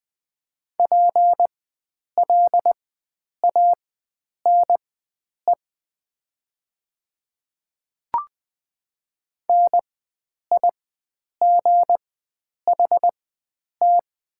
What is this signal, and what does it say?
Morse code, heard as a keyed tone, about 20 wpm, spaced as for 9 wpm, 700 Hz. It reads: PLANE NIGHT
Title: 4x Character Spacing - Top 500 Words - 20wpm